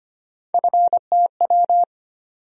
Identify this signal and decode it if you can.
Morse code, heard as a keyed tone, about 25 wpm, 700 Hz. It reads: FTW
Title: USA Call Sign Postfixes - 3 Letters 25wpm